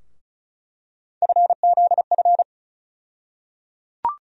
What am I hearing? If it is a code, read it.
Morse code, heard as a keyed tone, about 36 wpm, 700 Hz. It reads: FZF
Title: USA Call Sign Postfixes - 3 Letters 35wpm